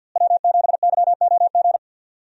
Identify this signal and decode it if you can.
Morse code, heard as a keyed tone, about 52 wpm, 700 Hz. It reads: W7YOG
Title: USA Call Signs - 5 Characters - Repeated 3x - 50wpm